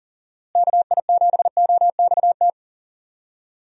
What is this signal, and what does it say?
Morse code, heard as a keyed tone, about 40 wpm, 700 Hz. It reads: KI7OXT
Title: USA Call Signs - 6 Characters - Repeated 2x - 40wpm